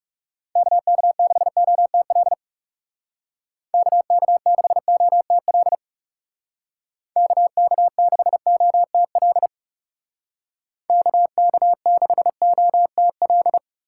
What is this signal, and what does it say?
Morse code, heard as a keyed tone, about 36 wpm, 700 Hz. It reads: KK6OTL KK6OTL KK6OTL KK6OTL
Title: Speed Racing - USA 2x3 Call Signs 30wpm